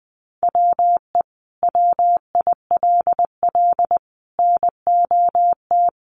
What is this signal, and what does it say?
Morse code, heard as a keyed tone, about 20 wpm, 700 Hz. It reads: WE WILL NOT
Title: Sentences from Top 1100 Words 20wpm